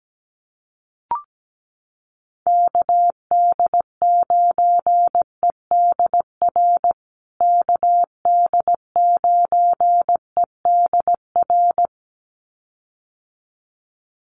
Morse code, heard as a keyed tone, about 17 wpm, 700 Hz. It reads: KD9EDR KD9EDR
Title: USA Call Signs Repeated 2x 17wpm